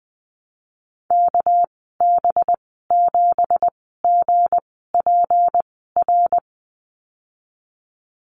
Morse code, heard as a keyed tone, about 19 wpm, spaced as for 14 wpm, 700 Hz. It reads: KB7GPR
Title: 207 Lesson 33g - Letter j - Call Signs 20wpm